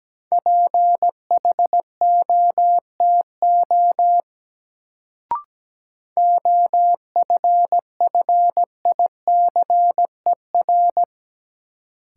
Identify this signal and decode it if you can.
Morse code, heard as a keyed tone, about 17 wpm, 700 Hz. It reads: PHOTO OFFICER
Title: Sentences from Top 1200 Words - New Word Drill Alt - 17wpm